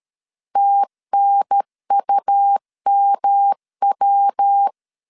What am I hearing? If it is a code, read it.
Morse code, heard as a keyed tone, about 13 wpm, 800 Hz. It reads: TNUMW
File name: CW / DET / 1 3 1500C S B 12wpm 2828